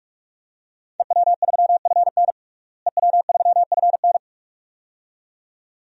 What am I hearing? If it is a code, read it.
Morse code, heard as a keyed tone, about 46 wpm, 700 Hz. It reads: EW3FN EW3FN
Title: DXCC-100 Call Signs - 5 Characters - Repeated 2x - 45wpm